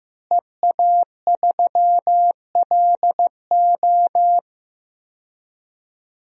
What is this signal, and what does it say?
Morse code, heard as a keyed tone, about 15 wpm, 700 Hz. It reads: EA3LO
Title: DXCC-100 Call Signs - 5 Characters - Repeated 3x - 15wpm